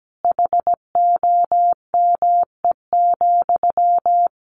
Morse code, heard as a keyed tone, about 17 wpm, 700 Hz. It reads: HOME,
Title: Sentences from Top 1400 Words 17wpm